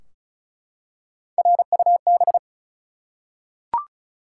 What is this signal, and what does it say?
Morse code, heard as a keyed tone, about 36 wpm, 700 Hz. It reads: RUB
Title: Three Letter Words - 35wpm